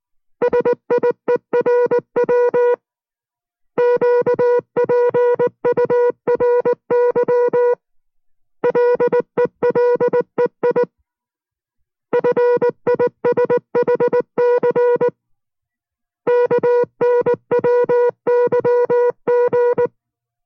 Morse code, heard as a keyed tone, about 19 wpm, 490 Hz. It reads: SIERW QPURY LELEI FISHC KNWYG